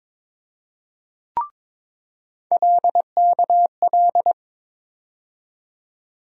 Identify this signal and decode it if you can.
Morse code, heard as a keyed tone, about 22 wpm, 700 Hz. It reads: LKL
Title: USA Call Sign Postfixes - 3 Letters 22wpm